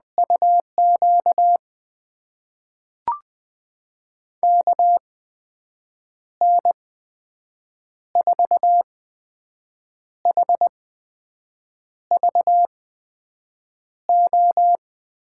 Morse code, heard as a keyed tone, about 20 wpm, spaced as for 6 wpm, 700 Hz. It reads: UQ KN4HVO
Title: USA Call Signs - 6 Characters - 8x Character Spacing - 20wpm